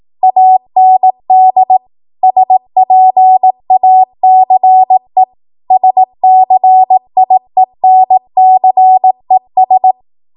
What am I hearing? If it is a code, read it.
Morse code, heard as a keyed tone, about 18 wpm, 750 Hz. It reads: AND SPACE SCIENCES